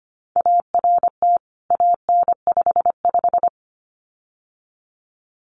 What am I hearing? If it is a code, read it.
Morse code, heard as a keyed tone, about 25 wpm, 700 Hz. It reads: ART AN55
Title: CWT Exchange 25wpm